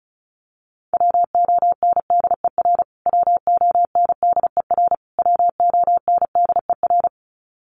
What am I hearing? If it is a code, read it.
Morse code, heard as a keyed tone, about 34 wpm, 700 Hz. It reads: WONDER WONDER WONDER
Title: Top 500 Words Repeated 35wpm